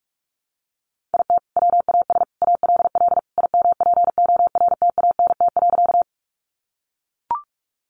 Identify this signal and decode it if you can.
Morse code, heard as a keyed tone, about 44 wpm, 700 Hz. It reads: IT WAS ALL IMPORTANT.
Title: Sentences from Top 600 Words 45wpm